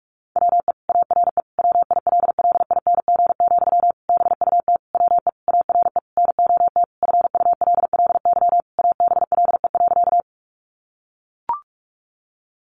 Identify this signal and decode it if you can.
Morse code, heard as a keyed tone, about 44 wpm, 700 Hz. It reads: WE ARE WILLING, BUT WE ARE NOT FULLY ABLE.
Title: Sentences from Top 1300 Words 45wpm